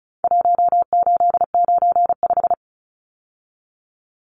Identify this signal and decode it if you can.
Morse code, heard as a keyed tone, about 34 wpm, 700 Hz. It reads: 1895
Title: Numbers - Contest Style - 4-Digit 35wpm